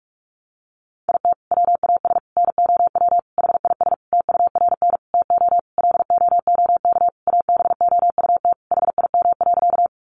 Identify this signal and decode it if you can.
Morse code, heard as a keyed tone, about 44 wpm, 700 Hz. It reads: IT WAS NOW HIS TURN TO LOOK ABOUT HIM.
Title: Sentences from Top 200 Words 45wpm